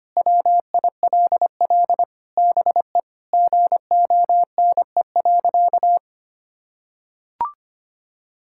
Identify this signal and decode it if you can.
Morse code, heard as a keyed tone, about 25 wpm, 700 Hz. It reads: WILL BE GONE.
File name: practice / 25wpm / Sentences with 4-Word Phrases - 25wpm